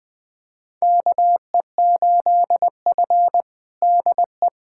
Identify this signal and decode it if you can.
Morse code, heard as a keyed tone, about 20 wpm, 700 Hz. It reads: KE8F DE
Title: USA Callsign DE Callsign - 4 Characters - 20wpm